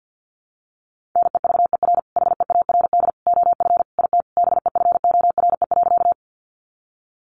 Morse code, heard as a keyed tone, about 50 wpm, 700 Hz. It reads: NEVER HEARD OF IT BEFORE.
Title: Sentences from Top 400 Words 50wpm